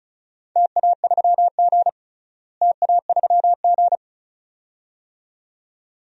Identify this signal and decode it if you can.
Morse code, heard as a keyed tone, about 36 wpm, 700 Hz. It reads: TA3G TA3G
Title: DXCC-100 Call Signs - 4 Characters - Repeated 2x - 35wpm